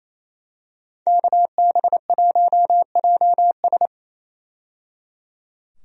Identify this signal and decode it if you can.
Morse code, heard as a keyed tone, about 28 wpm, 700 Hz. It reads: KB1JS
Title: USA Call Signs - 5 Characters - 28wpm